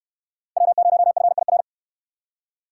Morse code, heard as a keyed tone, about 68 wpm, 700 Hz. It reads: W0FIR